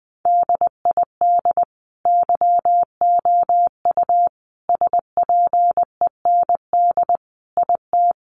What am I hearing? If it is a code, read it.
Morse code, heard as a keyed tone, about 19 wpm, 700 Hz. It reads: DID YOU SPEND IT
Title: Sentences from Top 800 Words 20wpm